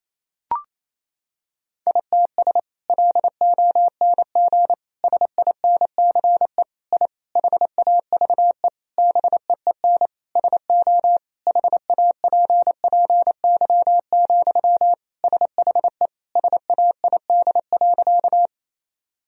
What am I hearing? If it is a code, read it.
Morse code, heard as a keyed tone, about 27 wpm, 700 Hz. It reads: ITS LONG SINCE I HAVE BEEN SO HAPPY, SHE SAID.
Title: Sentences from Top 1000 Words - Review Part 1 - 28wpm